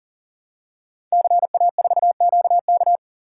Morse code, heard as a keyed tone, about 38 wpm, 690 Hz. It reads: CA4QK
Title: Code Groups - 5 Letter:Numbers - 40wpm